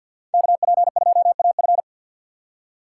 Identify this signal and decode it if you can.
Morse code, heard as a keyed tone, about 51 wpm, 700 Hz. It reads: KP2AF
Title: USA Call Signs - 5 Characters - 3x Character Spacing - 50wpm